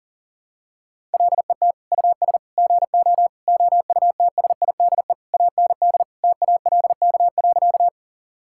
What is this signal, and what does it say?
Morse code, heard as a keyed tone, about 40 wpm, 700 Hz. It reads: LET US GO OUTSIDE AND TALK.
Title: Sentences from Top 1000 Words - Review Part 6 - 40wpm